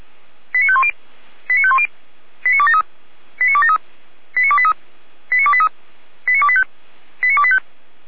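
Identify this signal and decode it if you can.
Five-tone selective call (ZVEI), heard as quick runs of five tones, tones of about 70 ms. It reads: 89510 89510 89273 89273 89283 89283 89285 89285
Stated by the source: ZVEI five-tone selective call received by SDR, recorded on the audio server of Feuerwehr Orlamünde (f85360).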